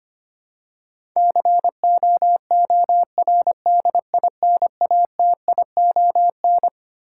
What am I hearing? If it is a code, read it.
Morse code, heard as a keyed tone, about 25 wpm, 700 Hz. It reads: COORDINATION
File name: Twelve Letter Words - 25wpm